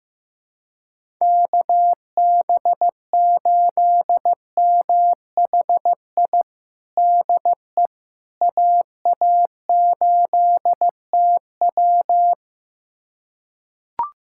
Morse code, heard as a keyed tone, about 15 wpm, 700 Hz. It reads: KB8MHI DE AA8TW
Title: USA Callsign DE Callsign 15wpm